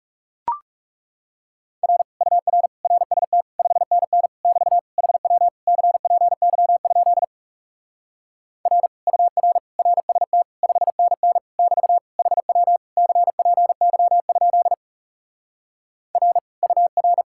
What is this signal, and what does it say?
Morse code, heard as a keyed tone, about 37 wpm, 700 Hz. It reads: R UR RST 5NN <BT> HW CPY? R UR RST 5NN <BT> HW CPY? R UR
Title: Speed-Racing Common QSO Phrases - 30wpm